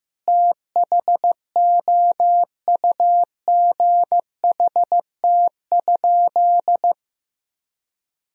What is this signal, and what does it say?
Morse code, heard as a keyed tone, about 15 wpm, 700 Hz. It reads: THOUGHT?
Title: Sentences from Top 300 Words and Abbreviations 15wpm